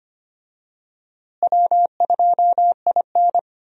Morse code, heard as a keyed tone, about 25 wpm, 700 Hz. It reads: W2IN
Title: USA Call Signs - 4 Characters - Repeated 3x - 25wpm